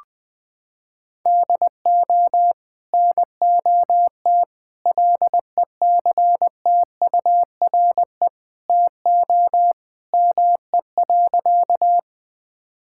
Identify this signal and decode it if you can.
Morse code, heard as a keyed tone, about 21 wpm, 700 Hz. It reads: DO NOT LECTURE TO ME.
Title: Sentences from Top 1600 Words 20wpm